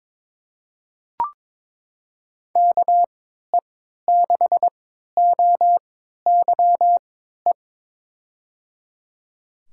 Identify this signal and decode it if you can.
Morse code, heard as a keyed tone, about 22 wpm, spaced as for 13 wpm, 700 Hz. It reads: KE6OYE
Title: USA Call Signs - 6 Characters - 3x Character Spacing - 22wpm